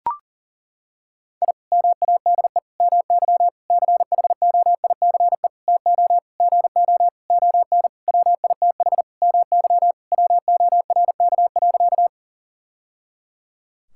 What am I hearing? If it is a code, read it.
Morse code, heard as a keyed tone, about 37 wpm, 700 Hz. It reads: I MADE MY CHOICE TO GO ON WITH MY WORK.
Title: Sentences from Top 1000 Words - Review Part 6 - 40wpm